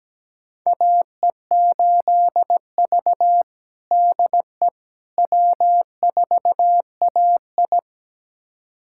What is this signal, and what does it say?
Morse code, heard as a keyed tone, about 17 wpm, 700 Hz. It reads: AE8V DE W4AI
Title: USA Callsign DE Callsign - 4 Characters - 17wpm